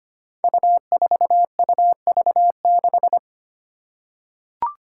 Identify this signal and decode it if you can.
Morse code, heard as a keyed tone, about 25 wpm, 700 Hz. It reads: U4UV6
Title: Five Challenging Letter-Number Combinations 25wpm